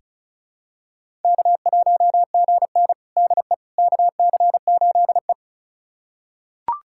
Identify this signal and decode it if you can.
Morse code, heard as a keyed tone, about 36 wpm, 700 Hz. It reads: K1GN DE KC8E
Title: USA Callsign DE Callsign - 4 Characters - 35wpm